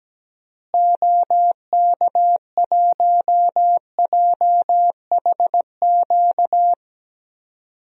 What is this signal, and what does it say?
Morse code, heard as a keyed tone, about 17 wpm, 700 Hz. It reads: OK1JHQ
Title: DXCC-100 Call Signs - 6 Characters - Repeated 2x - 17wpm